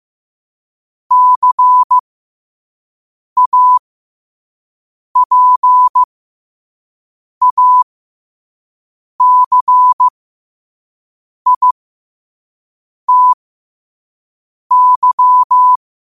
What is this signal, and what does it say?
Morse code, heard as a keyed tone, about 15 wpm, spaced as for 5 wpm, 1000 Hz. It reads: CAPACITY
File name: 5wpm